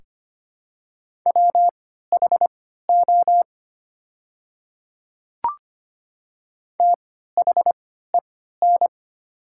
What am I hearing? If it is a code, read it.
Morse code, heard as a keyed tone, about 25 wpm, spaced as for 14 wpm, 700 Hz. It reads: WHO THEN
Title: 3x Character Spacing - Top 100 Words - 25wpm